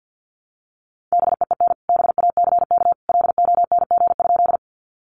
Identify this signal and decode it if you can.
Morse code, heard as a keyed tone, about 50 wpm, 700 Hz. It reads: BEEN BACK LONG?
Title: Sentences from Top 200 Words 50wpm